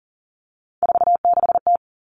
Morse code, heard as a keyed tone, about 40 wpm, 700 Hz. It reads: 46T
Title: All Cut Numbers - 3-Digit 40wpm